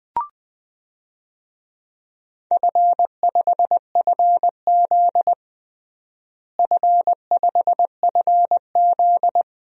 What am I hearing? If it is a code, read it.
Morse code, heard as a keyed tone, about 20 wpm, 700 Hz. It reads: F5FZ F5FZ
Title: DXCC-100 Call Signs - 4 Characters - Repeated 2x - 20wpm